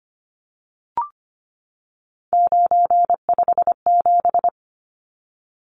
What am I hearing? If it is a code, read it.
Morse code, heard as a keyed tone, about 25 wpm, 700 Hz. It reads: 957
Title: Numbers - 3-Digit 25wpm